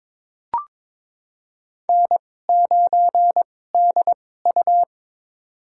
Morse code, heard as a keyed tone, about 22 wpm, spaced as for 16 wpm, 700 Hz. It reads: N9DU
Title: USA Call Signs - 4 Characters - 2x Character Spacing - 22wpm